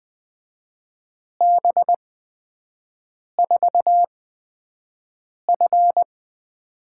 Morse code, heard as a keyed tone, about 20 wpm, spaced as for 5 wpm, 700 Hz. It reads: B4F
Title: USA Call Signs - 8x Character Spacing - 20wpm